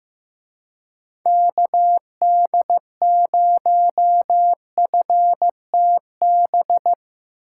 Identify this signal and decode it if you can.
Morse code, heard as a keyed tone, about 17 wpm, 700 Hz. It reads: KD0FTB
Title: Speed Racing - USA 2x3 Call Signs 15wpm